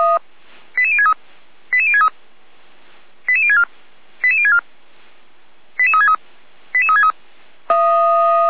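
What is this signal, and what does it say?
Five-tone selective call (ZVEI), heard as quick runs of five tones, tones of about 70 ms. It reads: 88063 88063 88064 88064 88363 88363
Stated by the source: ZVEI five-tone call received by SDR.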